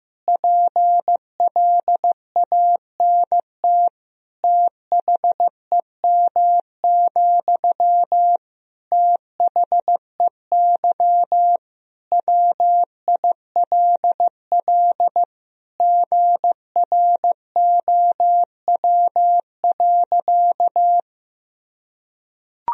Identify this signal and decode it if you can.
Morse code, heard as a keyed tone, about 15 wpm, 700 Hz. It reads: PLANT THEM, THEY WILL GROW.
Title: Sentences from Top 1000 Words - Review Part 1 - 15wpm